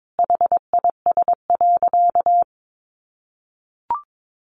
Morse code, heard as a keyed tone, about 22 wpm, 700 Hz. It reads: HIS.
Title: Sentences from Top 1000 Words 22wpm